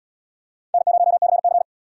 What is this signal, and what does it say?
Morse code, heard as a keyed tone, about 72 wpm, 700 Hz. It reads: N0CC